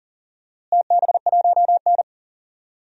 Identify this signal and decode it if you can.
Morse code, heard as a keyed tone, about 37 wpm, 700 Hz. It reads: TB1N
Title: DXCC-100 Call Signs - 4 Characters - Repeated 3x - 40wpm